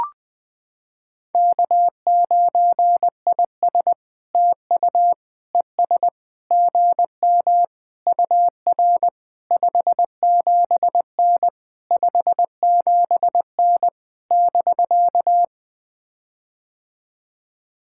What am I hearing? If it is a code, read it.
Morse code, heard as a keyed tone, about 20 wpm, 700 Hz. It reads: K9IS TU ES GM UR 57N 57N <BK>
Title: POTA QSO Practice - 20wpm